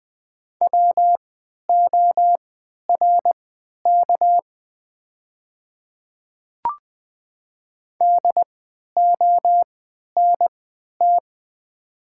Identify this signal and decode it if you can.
Morse code, heard as a keyed tone, about 20 wpm, spaced as for 11 wpm, 700 Hz. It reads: WORK DONT
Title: 3x Character Spacing - Top 300 Words - 20wpm